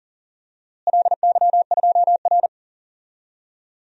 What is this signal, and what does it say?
Morse code, heard as a keyed tone, about 37 wpm, 700 Hz. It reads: LY2R
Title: DXCC-100 Call Signs - 4 Characters - Repeated 2x - 40wpm